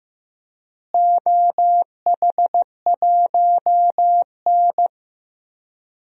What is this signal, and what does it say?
Morse code, heard as a keyed tone, about 15 wpm, 700 Hz. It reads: OH1N
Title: DXCC-100 Call Signs - 4 Characters - Repeated 3x - 15wpm